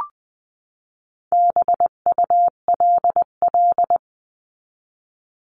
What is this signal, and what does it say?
Morse code, heard as a keyed tone, about 20 wpm, 700 Hz. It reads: BULL
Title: 259 Lesson 42d - Words 18-20wpm